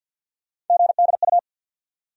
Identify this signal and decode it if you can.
Morse code, heard as a keyed tone, about 49 wpm, 700 Hz. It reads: GDU
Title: USA Call Sign Postfixes - 3 Letters 50wpm